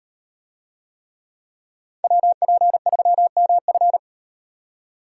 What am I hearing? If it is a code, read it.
Morse code, heard as a keyed tone, about 38 wpm, 700 Hz. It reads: WP3MF